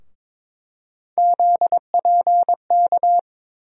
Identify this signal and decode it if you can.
Morse code, heard as a keyed tone, about 22 wpm, 700 Hz. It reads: ZPK
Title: USA Call Sign Postfixes - 3 Letters 22wpm